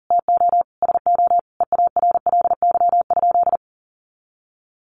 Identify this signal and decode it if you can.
Morse code, heard as a keyed tone, about 37 wpm, 700 Hz. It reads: TO SO EARLY?
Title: Sentences from Top 400 Words 40wpm